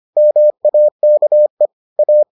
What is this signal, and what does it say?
Morse code, heard as a keyed tone, about 25 wpm, 600 Hz. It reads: MAKE A